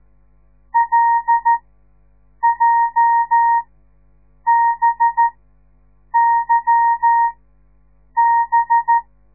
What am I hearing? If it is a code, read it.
Morse code, heard as a keyed tone, about 13 wpm, spaced as for 7 wpm, 930 Hz. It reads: LJBYB
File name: radiomorningstar / morse / les 9